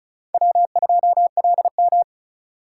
Morse code, heard as a keyed tone, about 45 wpm, 700 Hz. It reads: W2LM